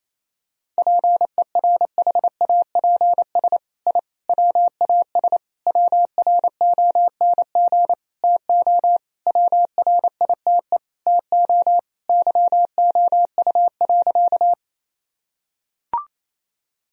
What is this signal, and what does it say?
Morse code, heard as a keyed tone, about 28 wpm, 700 Hz. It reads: PERHAPS I WAS WRONG TO WRITE TO YOU.
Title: Sentences from Top 1000 Words - Review Part 1 - 28wpm